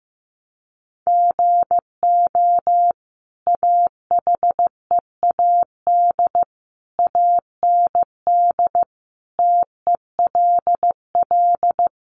Morse code, heard as a keyed tone, about 15 wpm, 700 Hz. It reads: GO AHEAD AND TELL
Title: Sentences from Top 1100 Words 15wpm